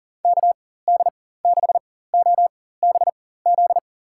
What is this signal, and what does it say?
Morse code, heard as a keyed tone, about 40 wpm, spaced as for 19 wpm, 700 Hz. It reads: KD6OBZ
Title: USA Call Signs - 6 Characters - 4x Character Spacing - 40wpm